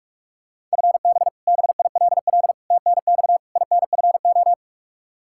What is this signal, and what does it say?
Morse code, heard as a keyed tone, about 41 wpm, 700 Hz. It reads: FB BILL TNX INFO
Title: Practice QSO - Exchange 3 - More Info - 45wpm